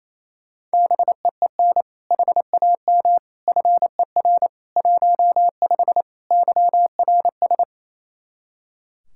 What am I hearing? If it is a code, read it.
Morse code, heard as a keyed tone, about 28 wpm, 700 Hz. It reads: BEEN HAM FER 15 YRS
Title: K1USN SST QSO Practice 28wpm